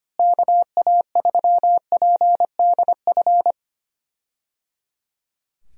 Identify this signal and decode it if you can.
Morse code, heard as a keyed tone, about 25 wpm, 700 Hz. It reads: KA3PDF
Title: USA Call Signs - 6 Characters - Repeated 3x - 25wpm